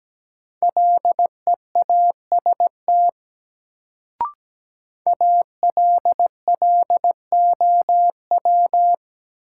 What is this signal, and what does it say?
Morse code, heard as a keyed tone, about 17 wpm, 700 Hz. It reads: LEAST ALLOW
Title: Sentences from Top 700 Words - New Word Drill - 17wpm